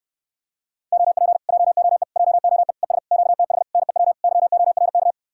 Morse code, heard as a keyed tone, about 78 wpm, 700 Hz. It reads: OP JOE JOE ES QTH NEW YORK